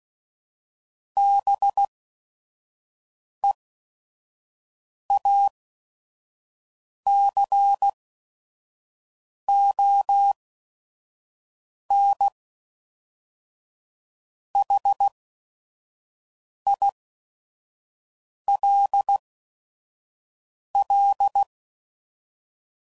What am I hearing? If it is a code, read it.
Morse code, heard as a keyed tone, about 16 wpm, spaced as for 5 wpm, 800 Hz. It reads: BEACONHILL
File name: CW Code Groups @5WPM / fromSM's / Cities